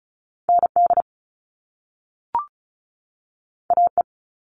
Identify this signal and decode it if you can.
Morse code, heard as a keyed tone, about 36 wpm, 700 Hz. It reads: ND AE
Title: Top Two Letter Combinations in Words 35wpm